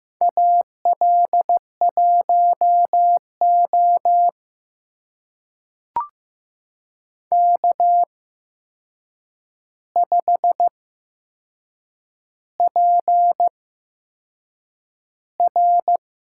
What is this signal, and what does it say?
Morse code, heard as a keyed tone, about 15 wpm, spaced as for 6 wpm, 700 Hz. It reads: AL1O K5PR
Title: USA Call Signs - 4 Characters - 8x Character Spacing - 15wpm